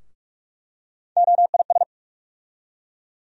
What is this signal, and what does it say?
Morse code, heard as a keyed tone, about 47 wpm, 700 Hz. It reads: OIS
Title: USA Call Sign Postfixes - 3 Letters 45wpm